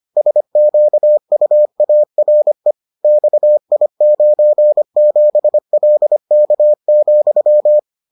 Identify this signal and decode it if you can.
Morse code, heard as a keyed tone, about 25 wpm, 600 Hz. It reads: SQUARE XI97LK,